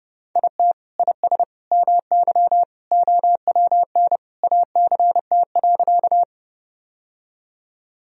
Morse code, heard as a keyed tone, about 30 wpm, 700 Hz. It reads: IT IS MY OWN ACT.
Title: Sentences from Top 1000 Words - Review Part 1 - 30wpm